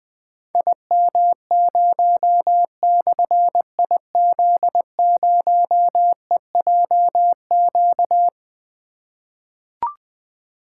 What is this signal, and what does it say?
Morse code, heard as a keyed tone, about 20 wpm, 700 Hz. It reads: IM0/IZ0EJQ
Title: Difficult Call Signs 20wpm